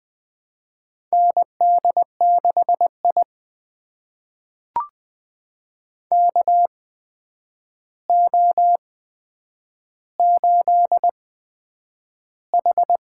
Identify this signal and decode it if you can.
Morse code, heard as a keyed tone, about 20 wpm, spaced as for 9 wpm, 700 Hz. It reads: ND6I KO8H
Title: USA Call Signs - 4 Characters - 8x Character Spacing - 20wpm